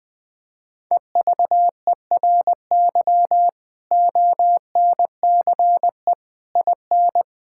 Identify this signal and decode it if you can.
Morse code, heard as a keyed tone, about 20 wpm, 700 Hz. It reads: EVERY ONCE IN
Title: Sentences with 5-Word Phrases - 20wpm